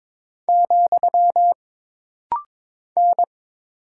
Morse code, heard as a keyed tone, about 22 wpm, 700 Hz. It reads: , N
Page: This rapid-fire timing is geared to push your ICR (Instant Character Recognition) skills.